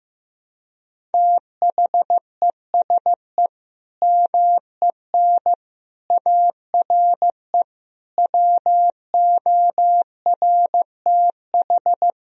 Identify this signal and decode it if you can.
Morse code, heard as a keyed tone, about 16 wpm, 700 Hz. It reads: THESE MEN ARE WORTH
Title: Sentences from Top 1000 Words - Review Part 1 - 15wpm